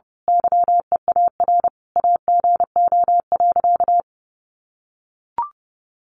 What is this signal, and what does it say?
Morse code, heard as a keyed tone, about 30 wpm, 700 Hz. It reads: YEAR AGO.
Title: Sentences from Top 1300 Words 30wpm